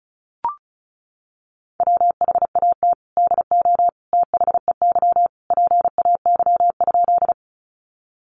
Morse code, heard as a keyed tone, about 34 wpm, 700 Hz. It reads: WHAT DO THEY PAY?
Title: Sentences from Top 700 Words 35wpm